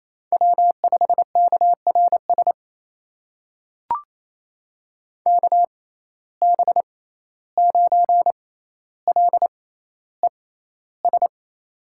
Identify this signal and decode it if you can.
Morse code, heard as a keyed tone, about 28 wpm, spaced as for 14 wpm, 700 Hz. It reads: W5KRS KB9LES
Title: USA Call Signs - 6x Character Spacing - 28wpm